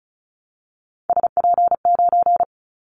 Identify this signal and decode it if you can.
Morse code, heard as a keyed tone, about 36 wpm, 700 Hz. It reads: SP9
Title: Callsign Prefixes for Top 25 DXCC - 35wpm